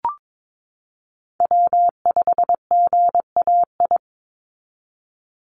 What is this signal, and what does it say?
Morse code, heard as a keyed tone, about 22 wpm, 700 Hz. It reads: W5GAI
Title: Call Signs Repeated 2x - 22wpm